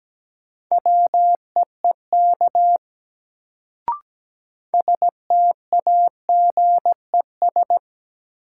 Morse code, heard as a keyed tone, about 17 wpm, 700 Hz. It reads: WEEK STAGES
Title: Sets of 5 Words Encore - New Word Drill - 17wpm